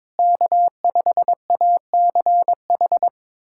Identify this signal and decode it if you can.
Morse code, heard as a keyed tone, about 22 wpm, 700 Hz. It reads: K5ACH
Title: USA Callsign DE Callsign - 5 Characters - 22wpm